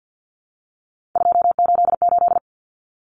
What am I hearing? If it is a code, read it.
Morse code, heard as a keyed tone, about 50 wpm, 700 Hz. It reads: 288
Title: Common Cut Numbers - 3-Digit 50wpm